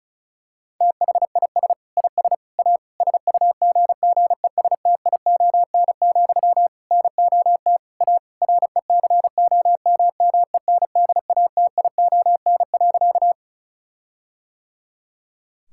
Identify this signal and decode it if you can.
Morse code, heard as a keyed tone, about 36 wpm, 700 Hz. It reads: THIS IS A SUGGESTION, NOT A RECOMMENDATION.
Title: Sentences from Top 1600 Words 35wpm